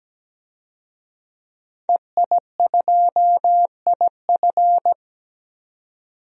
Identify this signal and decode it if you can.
Morse code, heard as a keyed tone, about 17 wpm, 700 Hz. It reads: EI2IF
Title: DXCC-100 Call Signs - 5 Characters - 17wpm